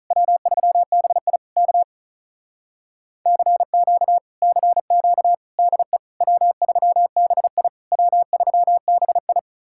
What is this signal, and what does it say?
Morse code, heard as a keyed tone, about 48 wpm, 700 Hz. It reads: W3BI K CQ CQ DE W3BI W3BI